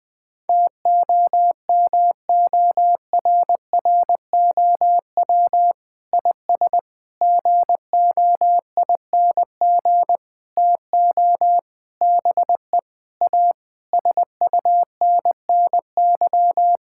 Morse code, heard as a keyed tone, about 21 wpm, 700 Hz. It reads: TOMORROW IS GOING TO BE A SUNNY
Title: Sentences with 5-Word Phrases - Spoken Beginning - 20wpm